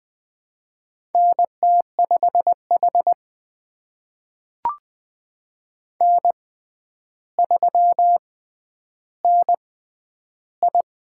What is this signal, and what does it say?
Morse code, heard as a keyed tone, about 20 wpm, spaced as for 10 wpm, 700 Hz. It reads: NT5H N3NI
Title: USA Call Signs - 4 Characters - 6x Character Spacing - 20wpm